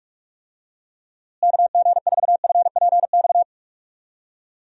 Morse code, heard as a keyed tone, about 46 wpm, 700 Hz. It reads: KG4FPX
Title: USA Call Signs - 6 Characters - Repeated 3x - 45wpm